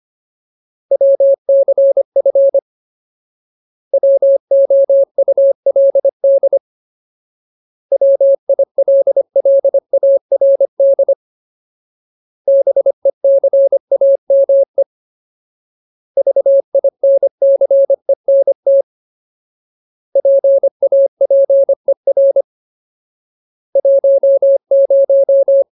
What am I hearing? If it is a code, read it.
Morse code, heard as a keyed tone, about 25 wpm, 550 Hz. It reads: WCF WOULD WILLARD BECAME VINCENT PAPER 10
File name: Ultimate Word List / 25WPM Ultimate